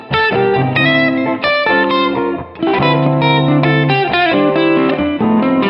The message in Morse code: none